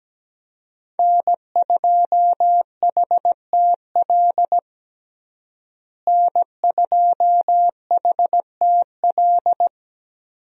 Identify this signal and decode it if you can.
Morse code, heard as a keyed tone, about 17 wpm, 700 Hz. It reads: N2HTL N2HTL
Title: USA Call Signs - 5 Characters - Repeated 3x - 17wpm